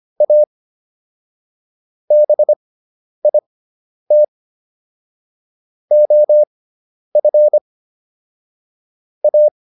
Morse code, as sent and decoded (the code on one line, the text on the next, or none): .- / -... .. - / --- ..-. / .-
A BIT OF A